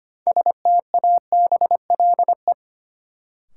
Morse code, heard as a keyed tone, about 25 wpm, 700 Hz. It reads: STABLE
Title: Sentences from Top 1500 Words - New Word Drill - 25wpm